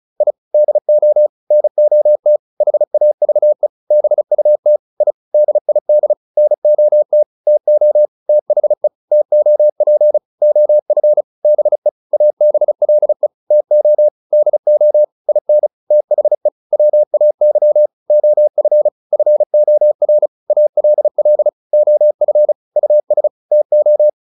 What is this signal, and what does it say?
Morse code, heard as a keyed tone, about 34 wpm, 600 Hz. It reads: I DO NOT HAVE BUT I DID NOT TO THE TOP OF BE ABLE TO DO IN THE WAY OF FOR ALL OF US TO
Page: advanced-practice-files